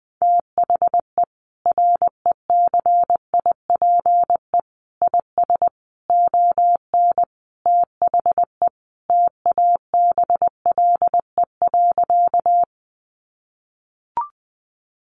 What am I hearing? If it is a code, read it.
Morse code, heard as a keyed tone, about 19 wpm, 700 Hz. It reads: THE RECIPE IS ON THE TABLE.
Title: Sentences from Top 1200 Words 20wpm